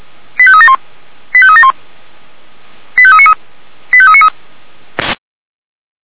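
Five-tone selective call (ZVEI), heard as quick runs of five tones, tones of about 70 ms. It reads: 85371 85371 85393 85393
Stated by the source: ZVEI five-tone selective call received on an SDR.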